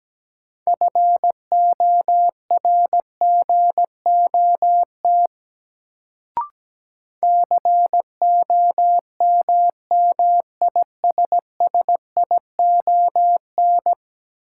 Morse code, heard as a keyed tone, about 17 wpm, 700 Hz. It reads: FORGOT COMMISSION
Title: Sentences from Top 1400 Words - New Word Drill Alt - 17wpm